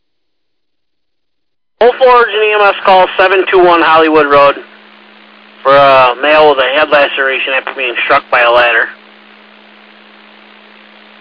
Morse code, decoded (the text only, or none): none